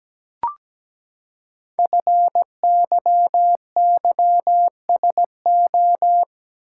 Morse code, heard as a keyed tone, about 17 wpm, 700 Hz. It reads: FYYSO
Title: Code Groups - 5 Letters - 17wpm